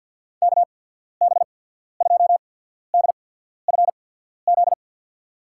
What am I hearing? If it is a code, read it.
Morse code, heard as a keyed tone, about 50 wpm, spaced as for 14 wpm, 700 Hz. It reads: KB2DFZ
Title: USA Call Signs - 6 Characters - 8x Character Spacing - 50wpm